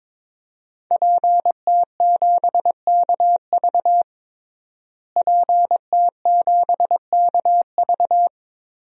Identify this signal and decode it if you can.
Morse code, heard as a keyed tone, about 22 wpm, 700 Hz. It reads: PT7KV PT7KV
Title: DXCC-100 Call Signs - 5 Characters - Repeated 3x - 22wpm